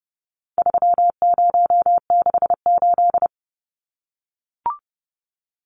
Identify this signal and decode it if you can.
Morse code, heard as a keyed tone, about 30 wpm, 700 Hz. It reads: 3068
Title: Numbers - 4-Digit 30wpm